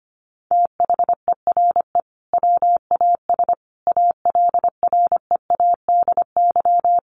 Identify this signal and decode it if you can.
Morse code, heard as a keyed tone, about 25 wpm, 700 Hz. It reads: THERE WAS ALREADY
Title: Sentences from Top 700 Words 25wpm